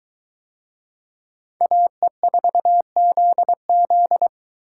Morse code, heard as a keyed tone, about 22 wpm, 700 Hz. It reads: AE4ZZ